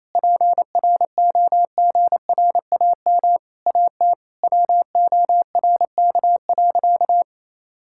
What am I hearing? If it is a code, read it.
Morse code, heard as a keyed tone, about 28 wpm, 700 Hz. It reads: PROGRAM AT WORK.
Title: Sentences from Top 1000 Words - Review Part 5 - 28wpm